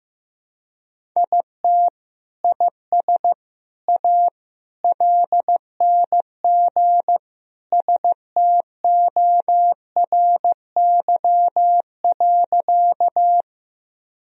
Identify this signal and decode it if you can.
Morse code, heard as a keyed tone, about 15 wpm, 700 Hz. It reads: IT IS A LNG STORY.
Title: Sentences from Top 300 Words and Abbreviations 15wpm